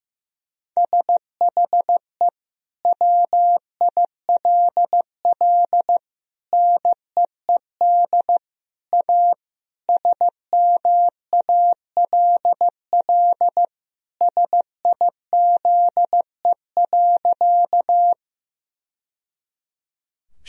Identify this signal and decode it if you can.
Morse code, heard as a keyed tone, about 15 wpm, 700 Hz. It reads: SHE WILL NEED A SMALL SIZE.
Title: Sentences from Top 1000 Words - Review Part 2 - 15wpm